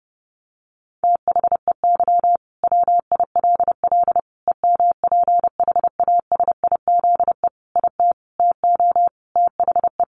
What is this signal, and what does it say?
Morse code, heard as a keyed tone, about 30 wpm, 700 Hz. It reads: THEY WILL EMPHASIZE IT TO THE
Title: Sentences from Top 1500 Words 30wpm